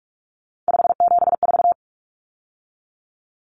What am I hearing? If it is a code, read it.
Morse code, heard as a keyed tone, about 44 wpm, 700 Hz. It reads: 574
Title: Numbers - 3-Digit 45wpm